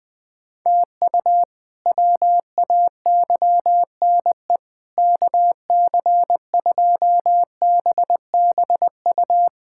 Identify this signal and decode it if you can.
Morse code, heard as a keyed tone, about 20 wpm, 700 Hz. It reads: TU WAYNE KC2BBU
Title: K1USN SST QSO Practice - 20wpm